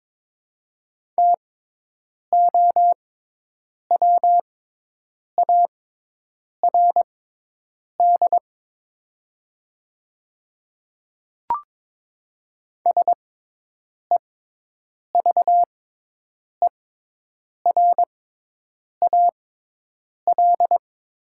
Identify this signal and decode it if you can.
Morse code, heard as a keyed tone, about 22 wpm, spaced as for 8 wpm, 700 Hz. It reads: TOWARD SEVERAL
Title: 6x Character Spacing - Top 500 Words - 22wpm